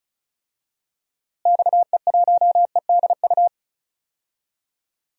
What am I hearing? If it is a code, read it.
Morse code, heard as a keyed tone, about 36 wpm, 700 Hz. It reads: XE1EDU
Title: DXCC-100 Call Signs - 6 Characters - Repeated 3x - 35wpm